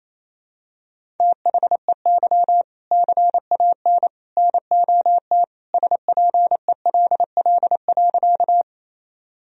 Morse code, heard as a keyed tone, about 28 wpm, 700 Hz. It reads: THEY CAN NOT SPELL.
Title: Sentences from Top 1000 Words - Review Part 1 - 28wpm